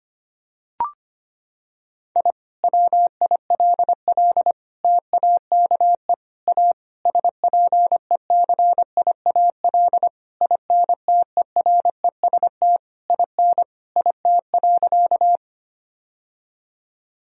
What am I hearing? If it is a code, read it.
Morse code, heard as a keyed tone, about 25 wpm, 700 Hz. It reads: I WILL TAKE A SPECIAL INTEREST IN IT.
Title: Sentences from Top 1000 Words - Review Part 3 - 25wpm